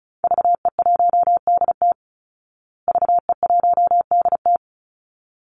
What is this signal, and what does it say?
Morse code, heard as a keyed tone, about 34 wpm, 700 Hz. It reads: VE1DT VE1DT
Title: Call Signs Repeated 3x - 35wpm